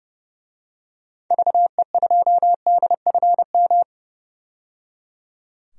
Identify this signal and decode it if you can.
Morse code, heard as a keyed tone, about 30 wpm, 700 Hz. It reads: VE2DFM
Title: DXCC-100 Call Signs - 6 Characters - 30wpm